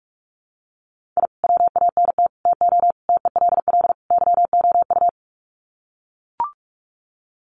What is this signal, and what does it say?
Morse code, heard as a keyed tone, about 44 wpm, 700 Hz. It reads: I WANT TO TELL YOU
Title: Sets of 5 Words Encore 45wpm